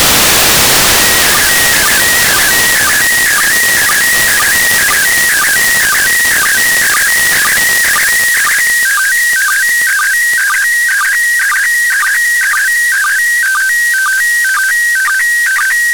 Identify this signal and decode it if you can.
SSTV (slow-scan television), PD 120.